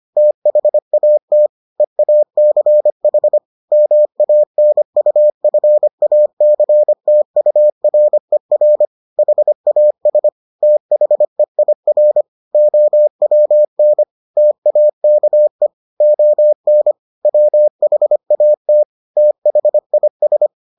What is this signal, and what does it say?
Morse code, heard as a keyed tone, about 25 wpm, 600 Hz. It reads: THAT EACH MANUFACTURER HAS THEIR OWN TAKE ON WHAT THIS